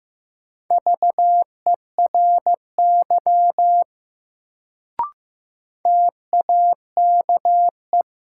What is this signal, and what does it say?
Morse code, heard as a keyed tone, about 15 wpm, 700 Hz. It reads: VERY TAKE
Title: Sentences from Top 200 Words - New Word Drill - 15wpm